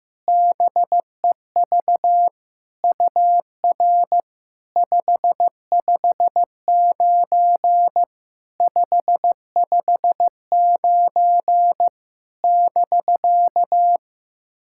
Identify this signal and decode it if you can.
Morse code, heard as a keyed tone, about 15 wpm, 700 Hz. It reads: BEV UR 559 559 <BK>
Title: SOTA QSO Practice 15wpm